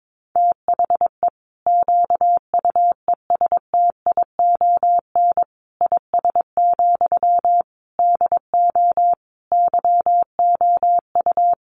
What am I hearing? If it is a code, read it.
Morse code, heard as a keyed tone, about 22 wpm, 700 Hz. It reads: THE QUESTION IS, DO YOU
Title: Sentences from Top 1000 Words 22wpm